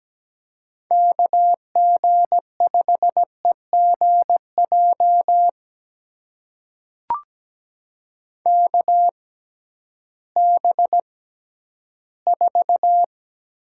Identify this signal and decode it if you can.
Morse code, heard as a keyed tone, about 17 wpm, spaced as for 11 wpm, 700 Hz. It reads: KG5EGJ KB4
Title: USA Call Signs - 6 Characters - 6x Character Spacing - 17wpm